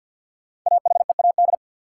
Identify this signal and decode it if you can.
Morse code, heard as a keyed tone, about 47 wpm, 700 Hz. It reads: AHEAD
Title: Sentences from Top 1100 Words - New Word Drill Alt - 50wpm